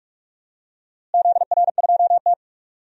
Morse code, heard as a keyed tone, about 46 wpm, 700 Hz. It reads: ZR2T
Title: DXCC-100 Call Signs - 4 Characters - Repeated 3x - 45wpm